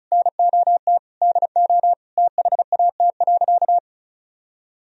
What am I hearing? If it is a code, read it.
Morse code, heard as a keyed tone, about 36 wpm, 700 Hz. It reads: NOT DO THAT.
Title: Sentences from Top 1000 Words - Review Part 4 - 35wpm